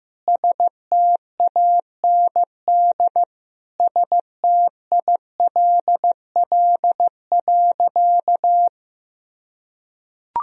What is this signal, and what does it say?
Morse code, heard as a keyed tone, about 15 wpm, 700 Hz. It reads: STAND STILL.
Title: Sentences from Top 1000 Words - Review Part 1 - 15wpm